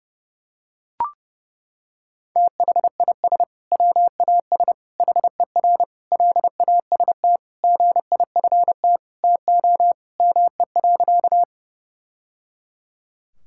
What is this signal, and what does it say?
Morse code, heard as a keyed tone, about 30 wpm, 700 Hz. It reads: THIS WAS HER LAST GIFT TO ME.
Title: Sentences from Top 1000 Words - Review Part 6 - 30wpm